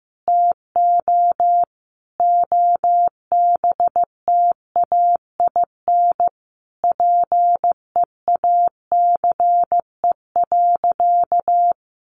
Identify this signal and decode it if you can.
Morse code, heard as a keyed tone, about 15 wpm, 700 Hz. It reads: TO OBTAIN PEACE.
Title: Sentences from Top 1500 Words 15wpm